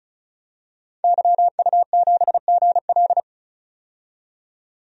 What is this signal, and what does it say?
Morse code, heard as a keyed tone, about 34 wpm, 700 Hz. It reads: YU7GL
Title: DXCC-100 Call Signs - 5 Characters - 35wpm